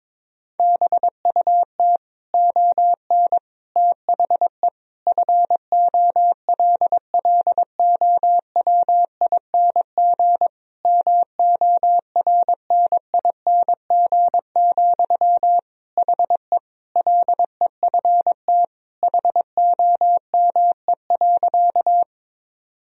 Morse code, heard as a keyed tone, about 22 wpm, 700 Hz. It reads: BUT ON THE FOLLOWING MORNING, HE LEFT HOME.
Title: Sentences from Top 1000 Words - Review Part 4 - 22wpm